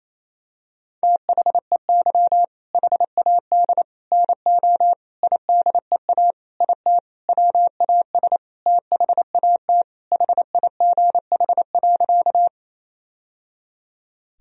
Morse code, heard as a keyed tone, about 27 wpm, 700 Hz. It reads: THEY HAD NO IDEA IT WAS THAT HIGH.
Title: Sentences from Top 1000 Words - Review Part 1 - 28wpm